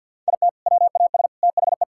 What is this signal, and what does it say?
Morse code, heard as a keyed tone, about 50 wpm, 700 Hz. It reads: IT WAS THE
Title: Sentences from Top 1000 Words - Review Part 6 - 50wpm